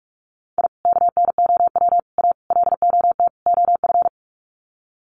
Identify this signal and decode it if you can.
Morse code, heard as a keyed tone, about 44 wpm, 700 Hz. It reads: I KNOW A LOT OF
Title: Sets of 5 Words Encore 45wpm